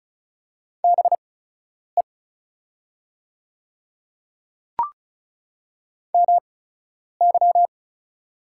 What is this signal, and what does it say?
Morse code, heard as a keyed tone, about 36 wpm, spaced as for 10 wpm, 700 Hz. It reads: BE MY
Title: 8x Character Spacing - Top 100 Words - 35wpm